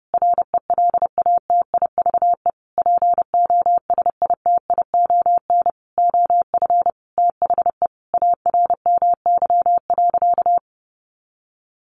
Morse code, heard as a keyed tone, about 30 wpm, 700 Hz. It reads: RELATIVE POSITION OF THE ARMY.
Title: Sentences from Top 1400 Words 30wpm